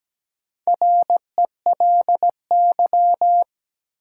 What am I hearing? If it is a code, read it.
Morse code, heard as a keyed tone, about 17 wpm, 700 Hz. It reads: RELY
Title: Sentences from Top 1200 Words - New Word Drill Alt - 17wpm